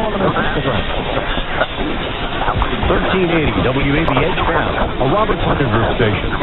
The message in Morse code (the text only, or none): none